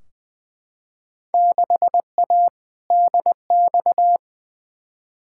Morse code, heard as a keyed tone, about 20 wpm, 700 Hz. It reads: 6A DX
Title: Field Day Exchanges Encore - 20wpm